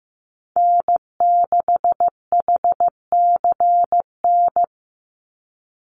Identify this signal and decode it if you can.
Morse code, heard as a keyed tone, about 15 wpm, 700 Hz. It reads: N6HCN
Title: Call Signs Repeated 2x - 15wpm